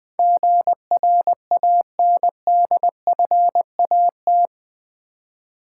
Morse code, heard as a keyed tone, about 20 wpm, 700 Hz. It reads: GRANDFAT
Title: Build A Word - 4 Letters At A Time - 11-Letter Words - 20wpm